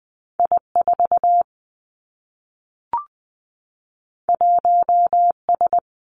Morse code, heard as a keyed tone, about 20 wpm, 700 Hz. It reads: I 4 1 S
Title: Two Challenging Letter-Number Combinations 20wpm